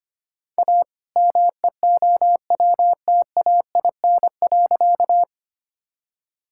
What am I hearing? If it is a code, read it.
Morse code, heard as a keyed tone, about 25 wpm, 700 Hz. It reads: A MEOWTAIN.
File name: practice / 25wpm / All-Age Appropriate Jokes - 25wpm